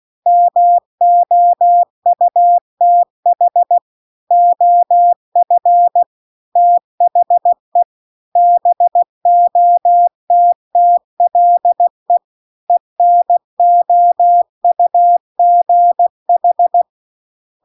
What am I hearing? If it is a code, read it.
Morse code, heard as a keyed tone, about 16 wpm, 700 Hz. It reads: MOUTH OF THE BOTTLE ENOUGH